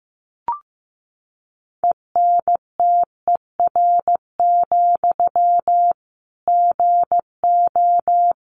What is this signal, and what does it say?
Morse code, heard as a keyed tone, about 15 wpm, 700 Hz. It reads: ENTER, GO
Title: Sentences from Top 1300 Words 15wpm